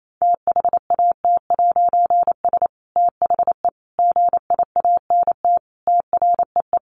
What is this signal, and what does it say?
Morse code, heard as a keyed tone, about 28 wpm, 700 Hz. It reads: THAT'S THE GIANT TREE
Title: A Sound Of Thunder 28wpm